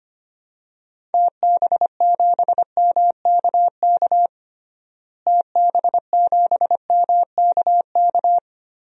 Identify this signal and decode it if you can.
Morse code, heard as a keyed tone, about 25 wpm, 700 Hz. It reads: TB7MKK TB7MKK
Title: DXCC-100 Call Signs - 6 Characters - Repeated 2x - 25wpm